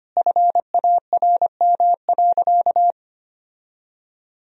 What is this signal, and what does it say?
Morse code, heard as a keyed tone, about 25 wpm, 700 Hz. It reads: FARM.
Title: Sentences from Top 1000 Words - Review Part 2 - 25wpm